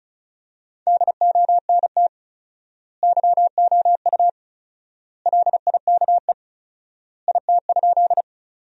Morse code, heard as a keyed tone, about 34 wpm, 700 Hz. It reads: DONT YOU LIKE IT?
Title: 4x Word-Spacing - Sentences from Top 300 Words - 35wpm